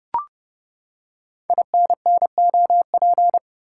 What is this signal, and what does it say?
Morse code, heard as a keyed tone, about 28 wpm, 700 Hz. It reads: INNOP
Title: Code Groups - 5 Letters - 30wpm